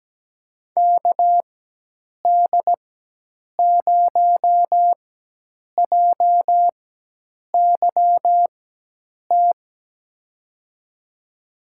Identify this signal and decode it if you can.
Morse code, heard as a keyed tone, about 17 wpm, spaced as for 8 wpm, 700 Hz. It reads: KD0JYT
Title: USA Call Signs - 6 Characters - 4x Character Spacing - 17wpm